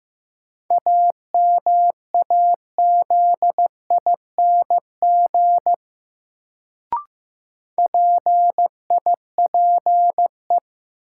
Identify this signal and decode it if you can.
Morse code, heard as a keyed tone, about 15 wpm, 700 Hz. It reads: AMAZING PIPE
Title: Sentences from Top 1500 Words - New Word Drill Alt - 15wpm